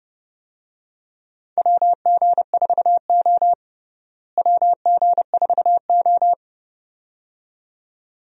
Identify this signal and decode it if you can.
Morse code, heard as a keyed tone, about 30 wpm, 700 Hz. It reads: WG4O WG4O
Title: USA Call Signs - 4 Characters - Repeated 2x - 30wpm